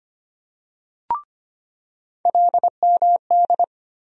Morse code, heard as a keyed tone, about 25 wpm, 700 Hz. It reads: LMD
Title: USA Call Sign Postfixes - 3 Letters 25wpm